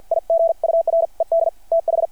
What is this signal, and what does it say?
Morse code, heard as a keyed tone, about 50 wpm, spaced as for 43 wpm, 660 Hz. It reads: IGURED TH
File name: archived / CRSnet